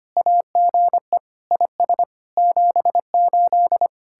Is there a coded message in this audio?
Yes: Morse code from 0.2 to 3.9 s, heard as a keyed tone, about 32 wpm, 700 Hz. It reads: AGE IS 78